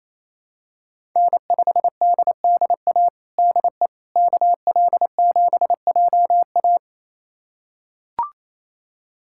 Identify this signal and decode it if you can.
Morse code, heard as a keyed tone, about 28 wpm, 700 Hz. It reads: N5DDA DE KL7JA
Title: USA Callsign DE Callsign - 5 Characters - 28wpm